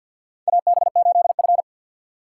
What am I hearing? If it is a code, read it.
Morse code, heard as a keyed tone, about 53 wpm, 700 Hz. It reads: AB8F
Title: USA Call Signs - 4 Characters - 2x Character Spacing - 50wpm